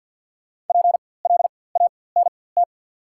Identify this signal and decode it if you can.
Morse code, heard as a keyed tone, about 54 wpm, spaced as for 24 wpm, 700 Hz. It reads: PLANT
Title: 4x Character Spacing - Top 500 Words - 50wpm